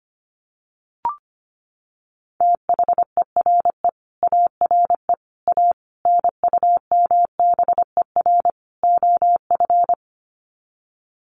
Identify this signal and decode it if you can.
Morse code, heard as a keyed tone, about 25 wpm, 700 Hz. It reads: THERE ARE A NUMBER OF
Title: Sets of 5 Words 25wpm